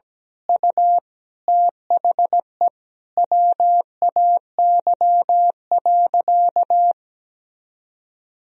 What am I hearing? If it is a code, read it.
Morse code, heard as a keyed tone, about 17 wpm, 700 Hz. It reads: U THE WAY.
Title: Sentences from Top 300 Words and Abbreviations 17wpm